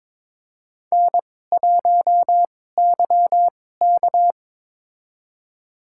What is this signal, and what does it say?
Morse code, heard as a keyed tone, about 22 wpm, spaced as for 16 wpm, 700 Hz. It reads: N1YK
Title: USA Call Signs - 4 Characters - 2x Character Spacing - 22wpm